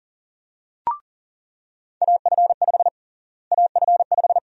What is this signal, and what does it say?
Morse code, heard as a keyed tone, about 37 wpm, 700 Hz. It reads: AF5 AF5
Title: USA Call Sign Prefixes - Repeated 2x - 40wpm